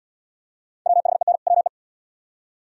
Morse code, heard as a keyed tone, about 69 wpm, 700 Hz. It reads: WHEN WE